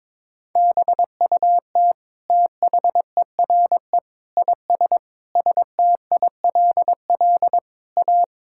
Morse code, heard as a keyed tone, about 22 wpm, 700 Hz. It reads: BUT THERE IS STILL A
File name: Sentences with 9-Word Phrases - 22wpm